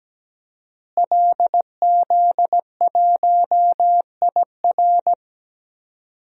Morse code, heard as a keyed tone, about 17 wpm, 700 Hz. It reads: LZ1IR
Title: DXCC-100 Call Signs - 5 Characters - Repeated 3x - 17wpm